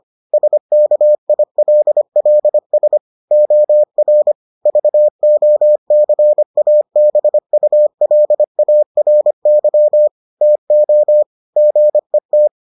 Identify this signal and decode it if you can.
Morse code, heard as a keyed tone, about 25 wpm, 600 Hz. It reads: SKILLS OR VOCABULARY TO GET